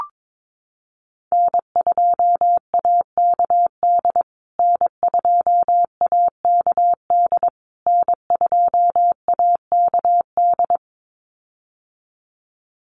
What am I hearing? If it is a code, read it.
Morse code, heard as a keyed tone, about 22 wpm, 700 Hz. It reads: N2AKD N2AKD N2AKD
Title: USA Call Signs Repeated 3x 22wpm